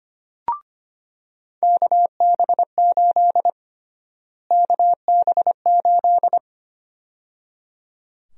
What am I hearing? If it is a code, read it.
Morse code, heard as a keyed tone, about 25 wpm, 700 Hz. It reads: KB8 KB8
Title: USA Call Sign Prefixes - Repeated 2x - 25wpm